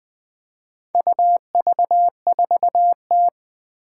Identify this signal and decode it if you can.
Morse code, heard as a keyed tone, about 20 wpm, 700 Hz. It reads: UV4T
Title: Four Challenging Letter-Number Combinations 20wpm